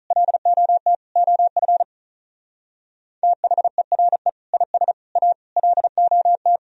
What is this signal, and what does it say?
Morse code, heard as a keyed tone, about 45 wpm, 700 Hz. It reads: LOT OF THERE IS A LOT